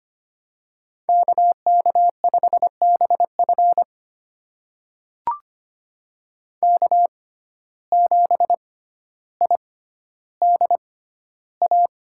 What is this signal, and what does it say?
Morse code, heard as a keyed tone, about 25 wpm, spaced as for 13 wpm, 700 Hz. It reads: KK5BF K7IDA
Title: USA Call Signs - 5 Characters - 6x Character Spacing - 25wpm